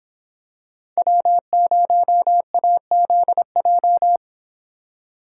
Morse code, heard as a keyed tone, about 26 wpm, 700 Hz. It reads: W0AZJ